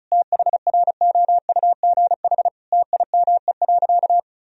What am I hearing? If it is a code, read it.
Morse code, heard as a keyed tone, about 34 wpm, 700 Hz. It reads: THROUGH TIME.
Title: Sentences from Top 1000 Words - Review Part 2 - 35wpm